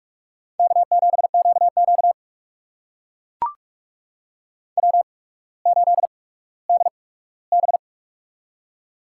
Morse code, heard as a keyed tone, about 44 wpm, spaced as for 19 wpm, 690 Hz. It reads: K7QQ W8DB
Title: USA Call Signs - 4 Characters - 8x Character Spacing - 45wpm